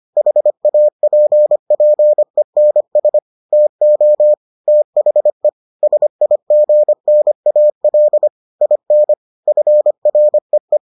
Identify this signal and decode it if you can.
Morse code, heard as a keyed tone, about 25 wpm, 600 Hz. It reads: HAPPENS TO THE SIGNAL IN FREE